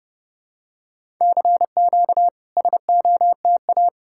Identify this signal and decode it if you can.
Morse code, heard as a keyed tone, about 30 wpm, 700 Hz. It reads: CQ SOTA
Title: SOTA QSO Practice 30wpm